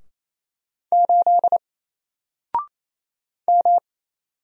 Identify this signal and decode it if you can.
Morse code, heard as a keyed tone, about 28 wpm, 700 Hz. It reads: 8 M
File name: Single-Letter-Number-28wpm